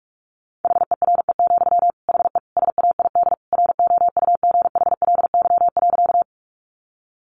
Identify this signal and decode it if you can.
Morse code, heard as a keyed tone, about 44 wpm, 700 Hz. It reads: HERE, HE SAID ROUGHLY.
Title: Sentences from Top 1500 Words 45wpm